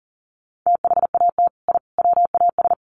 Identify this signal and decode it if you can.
Morse code, heard as a keyed tone, about 40 wpm, 700 Hz. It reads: THAT I WAS
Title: Sets of 3 Words 40wpm